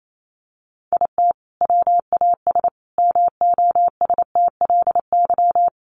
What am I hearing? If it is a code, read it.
Morse code, heard as a keyed tone, about 28 wpm, 700 Hz. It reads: IT WAS MOSTLY
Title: Sentences from Top 1300 Words 28wpm